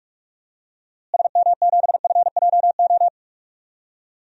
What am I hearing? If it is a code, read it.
Morse code, heard as a keyed tone, about 46 wpm, 700 Hz. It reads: SM7FJO
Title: DXCC-100 Call Signs - 6 Characters - Repeated 2x - 45wpm